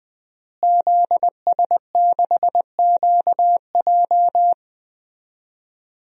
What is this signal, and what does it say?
Morse code, heard as a keyed tone, about 20 wpm, 700 Hz. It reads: ZS6QJ
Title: DXCC-100 Call Signs - 5 Characters - Repeated 2x - 20wpm